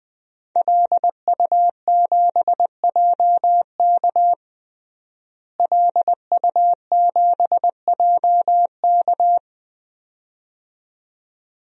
Morse code, heard as a keyed tone, about 21 wpm, 700 Hz. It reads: LU7JK LU7JK
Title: DXCC-100 Call Signs - 5 Characters - Repeated 2x - 20wpm